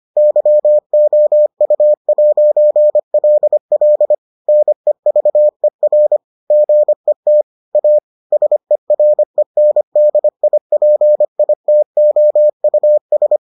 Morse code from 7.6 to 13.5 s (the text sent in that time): A SERENDIPITOUS